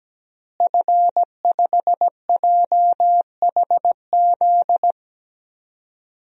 Morse code, heard as a keyed tone, about 17 wpm, 700 Hz. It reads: F5JHZ
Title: DXCC-100 Call Signs - 5 Characters - Repeated 2x - 17wpm